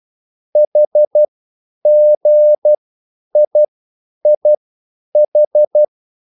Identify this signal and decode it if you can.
Morse code, heard as a keyed tone, about 12 wpm, spaced as for 9 wpm, 600 Hz. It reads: HGIIH